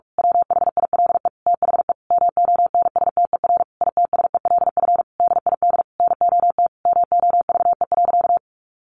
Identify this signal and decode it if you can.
Morse code, heard as a keyed tone, about 46 wpm, 700 Hz. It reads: WHILE THE MONSTER ITSELF DID NOT MOVE.
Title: A Sound Of Thunder 45wpm